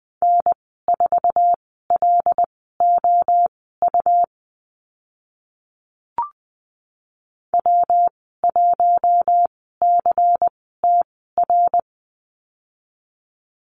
Morse code, heard as a keyed tone, about 20 wpm, spaced as for 14 wpm, 700 Hz. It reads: N4LOU W1CTR
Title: 140 Lesson 23g - Letter f - Call Signs 20wpm